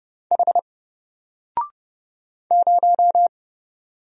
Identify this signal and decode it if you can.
Morse code, heard as a keyed tone, about 30 wpm, 700 Hz. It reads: H 0
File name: Single-Letter-Number-30wpm